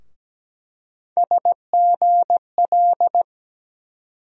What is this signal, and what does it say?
Morse code, heard as a keyed tone, about 17 wpm, 700 Hz. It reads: SGL
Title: USA Call Sign Postfixes - 3 Letters 17wpm